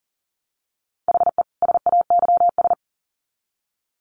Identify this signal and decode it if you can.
Morse code, heard as a keyed tone, about 40 wpm, 700 Hz. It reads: HE SAYS
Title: Sets of 2 Words 40wpm